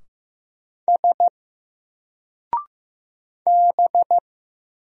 Morse code, heard as a keyed tone, about 15 wpm, 700 Hz. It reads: S B
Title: Single Letter-Number 15wpm